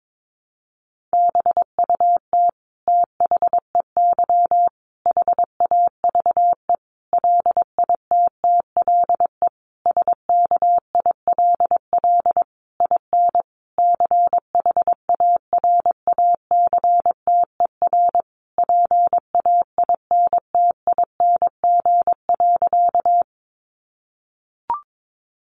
Morse code, heard as a keyed tone, about 22 wpm, 700 Hz. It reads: BUT THEY HAVE LITTLE SKILL IN CHARACTER PAINTING.
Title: Sentences from Top 1200 Words 22wpm